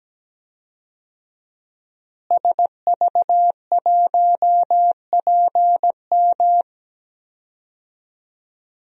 Morse code, heard as a keyed tone, about 17 wpm, 700 Hz. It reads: SV1PM
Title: DXCC-100 Call Signs - 5 Characters - Repeated 3x - 17wpm